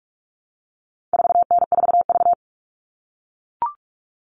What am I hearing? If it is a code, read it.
Morse code, heard as a keyed tone, about 44 wpm, 700 Hz. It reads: 4N4V
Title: All Cut Numbers - 4-Digit 45wpm